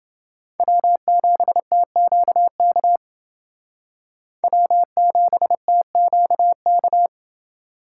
Morse code, heard as a keyed tone, about 28 wpm, 700 Hz. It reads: W7TQK W7TQK